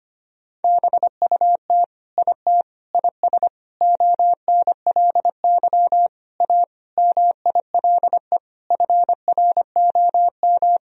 Morse code, heard as a keyed tone, about 25 wpm, 700 Hz. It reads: BUT IT IS ONLY A MILE FROM
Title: Sentences from Top 1000 Words - Review Part 1 - 25wpm